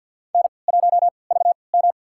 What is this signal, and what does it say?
Morse code, heard as a keyed tone, about 52 wpm, spaced as for 29 wpm, 700 Hz. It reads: N1VM
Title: USA Call Signs - 4 Characters - 3x Character Spacing - 50wpm